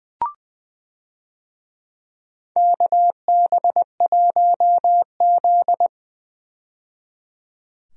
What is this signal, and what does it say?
Morse code, heard as a keyed tone, about 20 wpm, 700 Hz. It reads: KB1Z
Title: USA Call Signs - 4 Characters - 20wpm